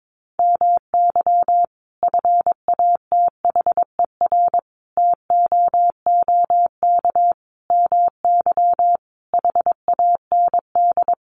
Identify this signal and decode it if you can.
Morse code, heard as a keyed tone, about 22 wpm, 700 Hz. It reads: MY FATHER TOOK MY HAND
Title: Sentences from Top 1500 Words 22wpm